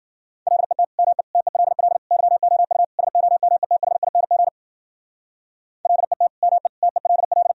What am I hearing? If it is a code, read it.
Morse code, heard as a keyed tone, about 52 wpm, 700 Hz. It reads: LET ME TELL YOU SOMETHING LET ME TELL